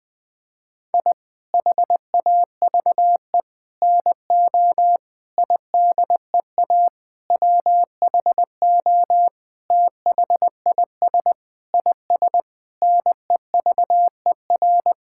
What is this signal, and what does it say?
Morse code, heard as a keyed tone, about 20 wpm, 700 Hz. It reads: I HAVE NO IDEA WHO THIS IS NEVER
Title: Sentences with 9-Word Phrases - Spoken Beginning - 20wpm